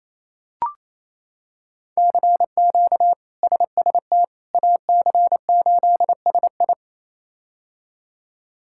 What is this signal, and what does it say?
Morse code, heard as a keyed tone, about 28 wpm, 700 Hz. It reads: CQ SST AC8SI
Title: K1USN SST QSO Practice - 28wpm